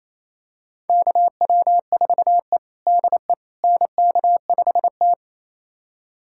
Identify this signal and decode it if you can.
Morse code, heard as a keyed tone, about 28 wpm, 700 Hz. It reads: KW4E DE NK5T
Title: USA Callsign DE Callsign - 4 Characters - 28wpm